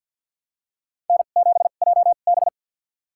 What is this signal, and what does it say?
Morse code, heard as a keyed tone, about 51 wpm, spaced as for 36 wpm, 700 Hz. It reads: N7JB
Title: USA Call Signs - 4 Characters - 2x Character Spacing - 50wpm